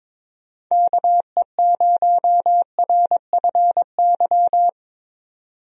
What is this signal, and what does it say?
Morse code, heard as a keyed tone, about 21 wpm, 700 Hz. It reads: KE0RFY
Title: USA Call Signs - 6 Characters - 22wpm